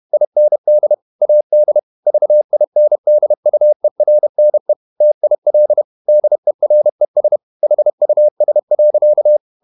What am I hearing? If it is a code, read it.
Morse code, heard as a keyed tone, about 31 wpm, 600 Hz. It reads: IND AD VINDUERNE TIL DERES HUS.
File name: Ferie QRQ 31wpm